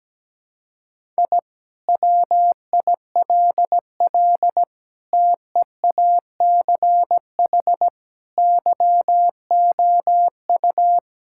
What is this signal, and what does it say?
Morse code, heard as a keyed tone, about 17 wpm, 700 Hz. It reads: I WILL TEACH YOU
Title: Sentences from Top 1000 Words - Review Part 6 - 17wpm